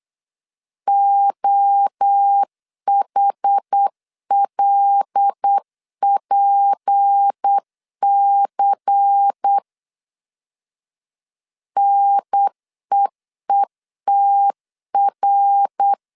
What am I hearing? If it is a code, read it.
Morse code, heard as a keyed tone, about 8 wpm, 800 Hz. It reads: OHLPC NEETR
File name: 8wpm